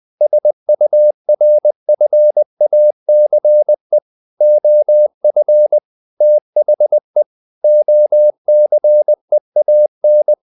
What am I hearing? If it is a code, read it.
Morse code, heard as a keyed tone, about 20 wpm, 600 Hz. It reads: SURFACE OF THE OCEAN
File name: the old man and the sea / the old man and the sea part 1 20wpm